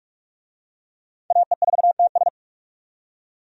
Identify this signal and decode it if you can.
Morse code, heard as a keyed tone, about 45 wpm, 700 Hz. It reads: AE4TS
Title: USA Call Signs - 5 Characters - 45wpm